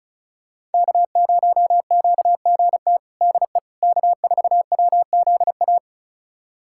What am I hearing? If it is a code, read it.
Morse code, heard as a keyed tone, about 36 wpm, 700 Hz. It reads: K0QGT DE K4WZA
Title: USA Callsign DE Callsign - 5 Characters - 35wpm